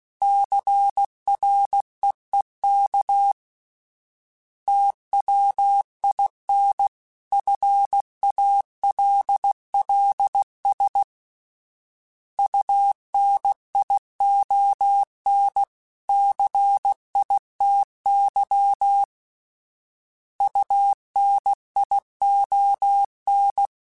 Morse code, heard as a keyed tone, about 16 wpm, 800 Hz. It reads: CREEK TWIN FALLS UNION CITY UNION
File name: fromSMs / FromSMs mp3s / Cities@15